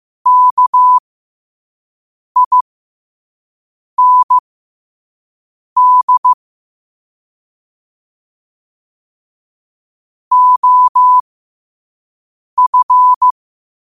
Morse code, heard as a keyed tone, about 15 wpm, spaced as for 5 wpm, 1000 Hz. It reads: KIND OF
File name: qotdfiles / 5wpm